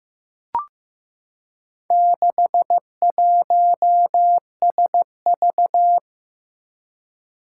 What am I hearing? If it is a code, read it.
Morse code, heard as a keyed tone, about 15 wpm, 700 Hz. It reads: 61SV
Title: Four Challenging Letter-Number Combinations 15wpm